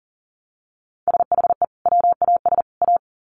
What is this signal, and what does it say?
Morse code, heard as a keyed tone, about 40 wpm, 700 Hz. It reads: SHE WAS A
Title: Sets of 3 Words 40wpm